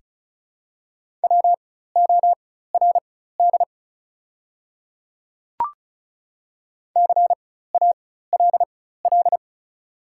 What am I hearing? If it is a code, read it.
Morse code, heard as a keyed tone, about 34 wpm, spaced as for 16 wpm, 700 Hz. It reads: WORD CALL
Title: 4x Character Spacing - Top 100 Words - 35wpm